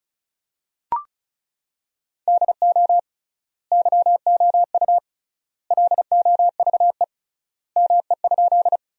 Morse code, heard as a keyed tone, about 34 wpm, 700 Hz. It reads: DO YOU LOVE ME?
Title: 3x Word-Spacing - Sentences from Top 400 Words - 35wpm